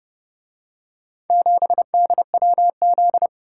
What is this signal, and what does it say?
Morse code, heard as a keyed tone, about 27 wpm, 700 Hz. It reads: 7DWZ